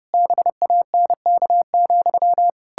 Morse code, heard as a keyed tone, about 30 wpm, 700 Hz. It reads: BANK,
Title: Pangram Challenge v2 - 30wpm